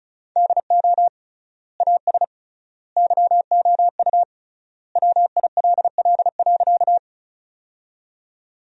Morse code, heard as a keyed tone, about 34 wpm, 700 Hz. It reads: DO AS YOU WILL.
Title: 3x Word-Spacing - Sentences from Top 100 Words - 35wpm